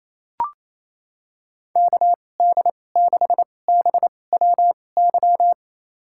Morse code, heard as a keyed tone, about 28 wpm, spaced as for 20 wpm, 700 Hz. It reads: KD6BWY
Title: USA Call Signs - 6 Characters - 2x Character Spacing - 28wpm